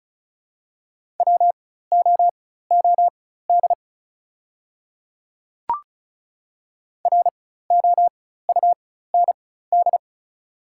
Morse code, heard as a keyed tone, about 34 wpm, spaced as for 16 wpm, 700 Hz. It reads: WOOD ROUND
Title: 4x Character Spacing - Top 300 Words - 35wpm